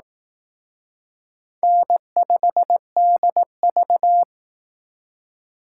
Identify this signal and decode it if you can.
Morse code, heard as a keyed tone, about 18 wpm, 700 Hz. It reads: N5DV
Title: Speed Racing - USA 1x2 Call Signs 15wpm